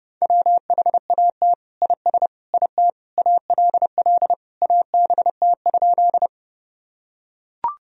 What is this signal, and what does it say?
Morse code, heard as a keyed tone, about 30 wpm, 700 Hz. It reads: WHAT IS IT ALL ABT?
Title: Sentences from Top 100 Words and Abbreviations 30wpm